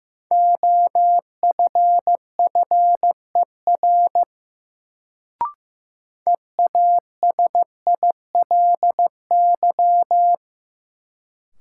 Morse code, heard as a keyed tone, about 15 wpm, 700 Hz. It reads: OFFER EASILY
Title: Sentences from Top 900 Words - New Word Drill - 15wpm